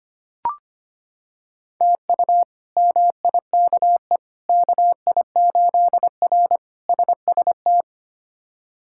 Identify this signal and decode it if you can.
Morse code, heard as a keyed tone, about 25 wpm, 700 Hz. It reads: TU MIKE KI8R SST
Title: K1USN SST QSO Practice - 25wpm